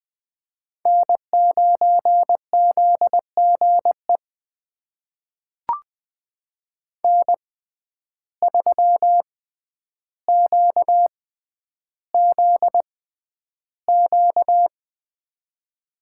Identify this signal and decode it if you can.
Morse code, heard as a keyed tone, about 20 wpm, spaced as for 10 wpm, 700 Hz. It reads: N9ZGE N3QZQ
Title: USA Call Signs - 5 Characters - 6x Character Spacing - 20wpm